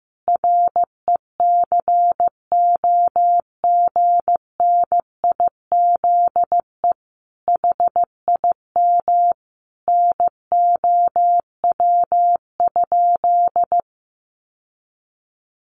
Morse code, heard as a keyed tone, about 15 wpm, 700 Hz. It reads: RECOGNIZE HIM NOW?
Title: Sentences from Top 1400 Words 15wpm